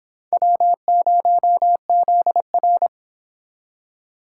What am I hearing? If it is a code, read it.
Morse code, heard as a keyed tone, about 32 wpm, 700 Hz. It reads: W0ZR